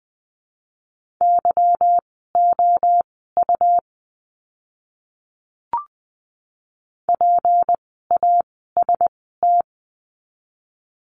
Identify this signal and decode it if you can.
Morse code, heard as a keyed tone, about 19 wpm, spaced as for 14 wpm, 700 Hz. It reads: YOU PAST
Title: 153 Lesson 25f - Letter p - Words 20wpm